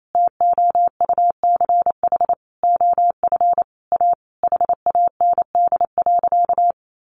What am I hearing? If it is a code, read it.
Morse code, heard as a keyed tone, about 28 wpm, 700 Hz. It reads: TOUCH OF A HAND.
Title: A Sound Of Thunder 28wpm